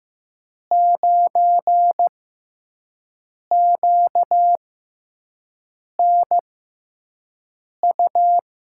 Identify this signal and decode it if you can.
Morse code, heard as a keyed tone, about 15 wpm, spaced as for 5 wpm, 700 Hz. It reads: 9QNU
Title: USA Call Signs - 6 Characters - 6x Character Spacing - 15wpm